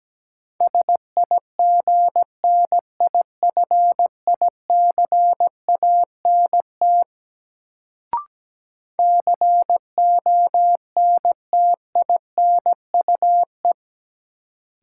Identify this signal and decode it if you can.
Morse code, heard as a keyed tone, about 17 wpm, 700 Hz. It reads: SIGNIFICANT CONTINUE
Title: Sentences from Top 1000 Words - Review Part 6 - New Word Drill - 17wpm